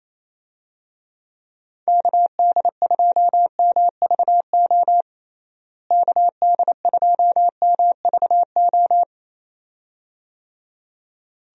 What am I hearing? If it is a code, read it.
Morse code, heard as a keyed tone, about 28 wpm, 700 Hz. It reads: KD2MVO KD2MVO
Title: USA Call Signs - 6 Characters - Repeated 2x - 28wpm